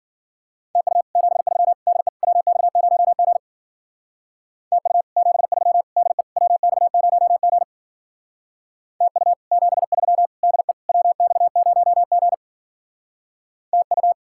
Morse code, heard as a keyed tone, about 49 wpm, 700 Hz. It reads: TU 73 DE WX0G TU 73 DE WX0G TU 73 DE WX0G TU
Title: Speed-Racing Common QSO Phrases - 40wpm